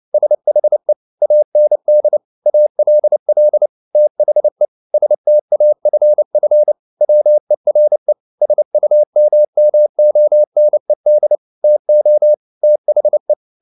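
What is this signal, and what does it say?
Morse code, heard as a keyed tone, about 29 wpm, 600 Hz. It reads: SHE AND ALL THE STAFF WERE SUMMONED TO THE
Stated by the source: Morse code at 29 wpm, sending Ken Follett's novel Never in English, from CW med Gnister.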